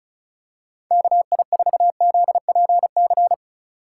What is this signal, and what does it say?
Morse code, heard as a keyed tone, about 36 wpm, 700 Hz. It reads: KI4ZPC
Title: USA Call Signs - 6 Characters - 6x Character Spacing - 35wpm